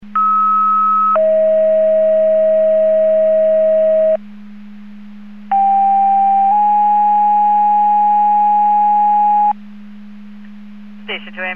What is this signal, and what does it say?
Two-tone page, heard as a one-second tone, then a three-second tone, about 1290 then 640 Hz, about 790 then 830 Hz. It is heard from 0.2 to 9.5 s.
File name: Portage / 06910 19-27 Portage 154.385 Paging